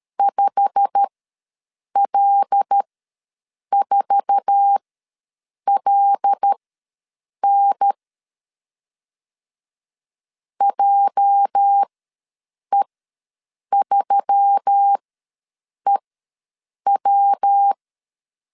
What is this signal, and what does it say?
Morse code, heard as a keyed tone, about 12 wpm, spaced as for 7 wpm, 800 Hz. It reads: 5L4LN JE3EW
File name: LEZIONI / 9 2867 ws 12wpm 3431